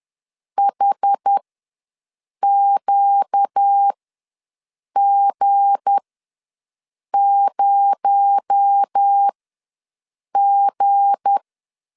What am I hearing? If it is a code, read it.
Morse code, heard as a keyed tone, about 10 wpm, spaced as for 6 wpm, 800 Hz. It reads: HQG0G
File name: LEZIONI / 7 GF9Z / 7 G ws 10wpm 3843